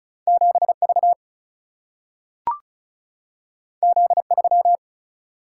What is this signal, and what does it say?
Morse code, heard as a keyed tone, about 36 wpm, 700 Hz. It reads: 7V Z3
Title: Single-Letter Single-Number 35wpm